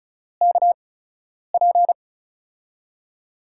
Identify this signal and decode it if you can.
Morse code, heard as a keyed tone, about 36 wpm, spaced as for 10 wpm, 700 Hz. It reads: KP
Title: USA Call Signs - 4 Characters - 8x Character Spacing - 35wpm